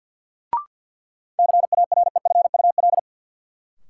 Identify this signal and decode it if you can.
Morse code, heard as a keyed tone, about 51 wpm, 700 Hz. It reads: CAREFUL
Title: Sentences from Top 1600 Words - New Word Drill - 50wpm